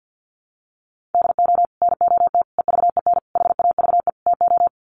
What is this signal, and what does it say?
Morse code, heard as a keyed tone, about 48 wpm, 700 Hz. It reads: DO NOT EVEN HAVE TO
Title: Sets of 5 Words Encore2 50wpm